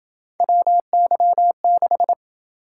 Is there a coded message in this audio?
Yes: Morse code from 0.4 to 2.1 s, heard as a keyed tone, about 28 wpm, 700 Hz. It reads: WY6